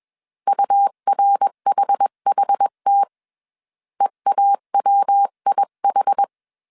Morse code, heard as a keyed tone, about 21 wpm, 800 Hz. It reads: URHHT EAWIH
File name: 20wpm